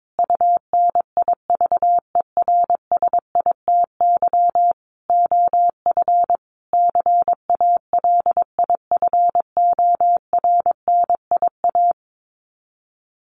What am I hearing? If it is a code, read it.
Morse code, heard as a keyed tone, about 22 wpm, 700 Hz. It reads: UNIVERSITY OF CALIFORNIA
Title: Sets of 5 Words Encore2 22wpm